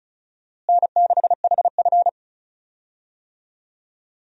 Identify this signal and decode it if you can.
Morse code, heard as a keyed tone, about 36 wpm, 700 Hz. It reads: N6HF
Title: USA Call Signs - 4 Characters - Repeated 2x - 35wpm